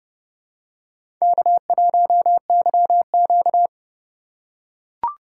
Morse code, heard as a keyed tone, about 28 wpm, 690 Hz. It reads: K1YQ
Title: USA Call Signs - 4 Characters - 2x Character Spacing - 30wpm